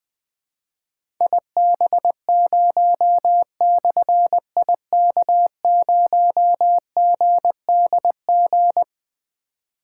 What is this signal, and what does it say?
Morse code, heard as a keyed tone, about 20 wpm, 700 Hz. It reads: IB0/IK0GDG
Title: Difficult Call Signs 20wpm